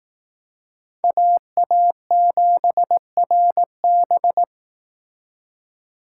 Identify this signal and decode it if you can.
Morse code, heard as a keyed tone, about 19 wpm, 700 Hz. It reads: AA7RB